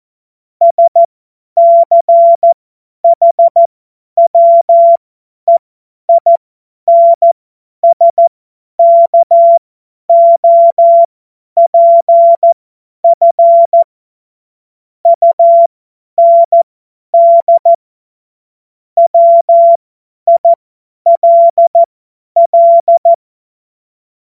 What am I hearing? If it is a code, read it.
Morse code, heard as a keyed tone, about 14 wpm, spaced as for 10 wpm, 680 Hz. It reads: SCHWEINSKOPF UND WILL
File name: Witz-19-August-2025-50-F70